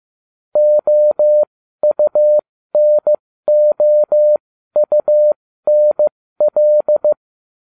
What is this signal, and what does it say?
Morse code, heard as a keyed tone, about 15 wpm, spaced as for 13 wpm, 600 Hz. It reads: OUNOUNL
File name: cw / ubacw13 2